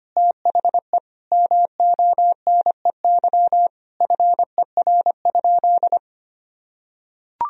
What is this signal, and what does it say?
Morse code, heard as a keyed tone, about 25 wpm, 700 Hz. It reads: THE MONEY FER?
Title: Sentences from Top 400 Words and Abbreviations 25wpm